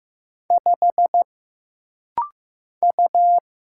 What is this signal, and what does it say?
Morse code, heard as a keyed tone, about 15 wpm, 700 Hz. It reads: 5 U
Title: Single Letter-Number Alt - Rapid-Fire 15wpm